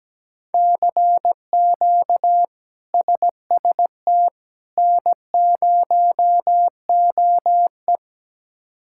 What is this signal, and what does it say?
Morse code, heard as a keyed tone, about 17 wpm, 700 Hz. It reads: CQ SST N0OE
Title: K1USN SST QSO Practice - 17wpm